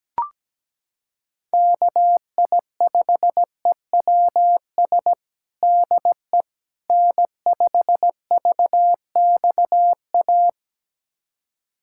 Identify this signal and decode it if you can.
Morse code, heard as a keyed tone, about 17 wpm, 700 Hz. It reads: KI5EWS DE N5VXA
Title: USA Callsign DE Callsign 17wpm